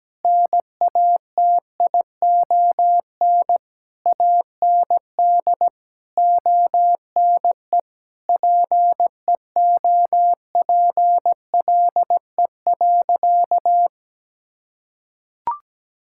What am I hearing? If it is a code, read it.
Morse code, heard as a keyed tone, about 17 wpm, 700 Hz. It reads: NATION AND ONE PEOPLE.
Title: Sentences from Top 1000 Words - Review Part 6 - 17wpm